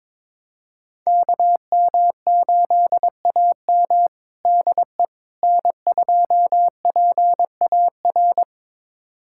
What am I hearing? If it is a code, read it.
Morse code, heard as a keyed tone, about 22 wpm, 700 Hz. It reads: KM8AM DE N2PAR
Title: USA Callsign DE Callsign - 5 Characters - 22wpm